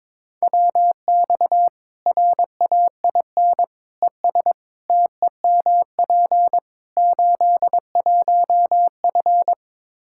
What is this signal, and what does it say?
Morse code, heard as a keyed tone, about 22 wpm, 700 Hz. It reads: WX RAIN ES TEMP 81F
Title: Practice QSO - Exchange 2 - Info - 22wpm